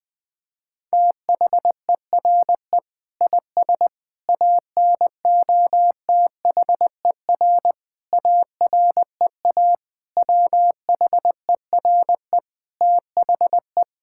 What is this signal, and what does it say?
Morse code, heard as a keyed tone, about 21 wpm, 700 Hz. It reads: THERE IS ANOTHER AREA WHERE THE
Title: Sentences from Top 1000 Words - Review Part 5 - 20wpm